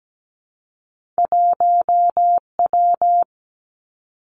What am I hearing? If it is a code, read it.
Morse code, heard as a keyed tone, about 17 wpm, 700 Hz. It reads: 1W
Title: Two Challenging Letter-Number Combinations 17wpm